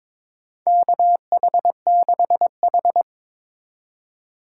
Morse code, heard as a keyed tone, about 22 wpm, 700 Hz. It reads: KH6H
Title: USA Call Signs - 4 Characters - 3x Character Spacing - 22wpm